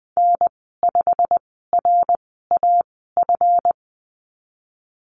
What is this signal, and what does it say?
Morse code, heard as a keyed tone, about 20 wpm, spaced as for 14 wpm, 700 Hz. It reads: N5RAF
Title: 147 Lesson 24g - Letter y - Call Signs 20wpm